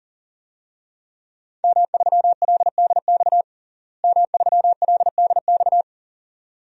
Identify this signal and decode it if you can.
Morse code, heard as a keyed tone, about 40 wpm, 700 Hz. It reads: M3LDX M3LDX
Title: DXCC-100 Call Signs - 5 Characters - Repeated 2x - 40wpm